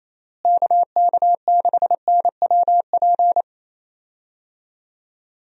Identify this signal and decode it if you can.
Morse code, heard as a keyed tone, about 28 wpm, 700 Hz. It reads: KK6NWP
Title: USA Call Signs - 6 Characters - Repeated 2x - 28wpm